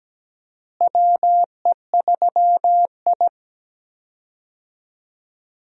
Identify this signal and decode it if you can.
Morse code, heard as a keyed tone, about 17 wpm, 700 Hz. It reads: WE3I
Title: USA Call Signs - 4 Characters - Repeated 3x - 17wpm